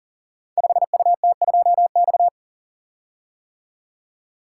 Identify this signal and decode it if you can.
Morse code, heard as a keyed tone, about 41 wpm, 690 Hz. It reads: 5UT2X
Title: Code Groups - 5 Letter:Numbers - 40wpm